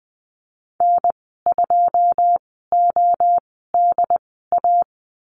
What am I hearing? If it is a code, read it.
Morse code, heard as a keyed tone, about 20 wpm, spaced as for 14 wpm, 700 Hz. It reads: N2ODA
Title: 088 Lesson 15g - Number 5 - Call Signs 20wpm